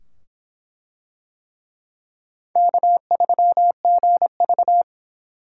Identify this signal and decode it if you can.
Morse code, heard as a keyed tone, about 24 wpm, 700 Hz. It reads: K3GV